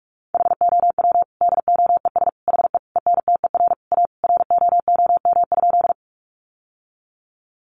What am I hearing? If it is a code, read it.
Morse code, heard as a keyed tone, about 46 wpm, 700 Hz. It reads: HOW DOES HE ENTER A ROOM?
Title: Sentences from Top 1300 Words 45wpm